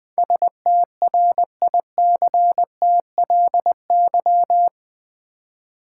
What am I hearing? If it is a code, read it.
Morse code, heard as a keyed tone, about 20 wpm, 700 Hz. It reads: STRICTLY
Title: Build A Word - 4 Letters At A Time - 8-Letter Words - 20wpm